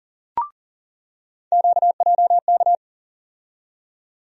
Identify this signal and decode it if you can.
Morse code, heard as a keyed tone, about 40 wpm, 700 Hz. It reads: QJK
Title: USA Call Sign Postfixes - 3 Letters 40wpm